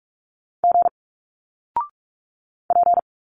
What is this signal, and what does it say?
Morse code, heard as a keyed tone, about 46 wpm, 700 Hz. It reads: G P
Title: Single Letters 45wpm